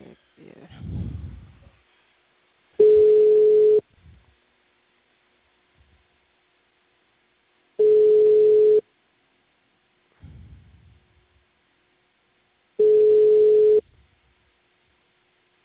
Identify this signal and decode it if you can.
Ringback tone (Europe, 425 Hz).